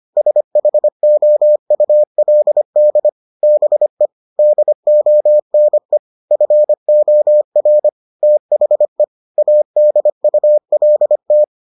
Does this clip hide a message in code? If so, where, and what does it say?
Yes: Morse code from 0.2 to 11.4 s, heard as a keyed tone, about 25 wpm, 600 Hz. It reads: SHOULD BE DONE FOR THE ADULT